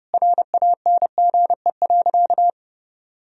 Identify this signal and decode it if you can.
Morse code, heard as a keyed tone, about 30 wpm, 700 Hz. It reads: RANGE.
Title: Sentences from Top 1000 Words - Review Part 4 - 30wpm